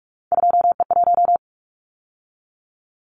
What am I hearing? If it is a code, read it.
Morse code, heard as a keyed tone, about 46 wpm, 700 Hz. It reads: 2E1
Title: Callsign Prefixes for Top 25 DXCC - 45wpm